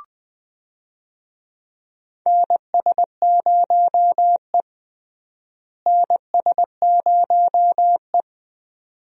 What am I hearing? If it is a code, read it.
Morse code, heard as a keyed tone, about 20 wpm, 700 Hz. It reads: NS0E NS0E
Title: USA Call Signs - 4 Characters - Repeated 2x - 20wpm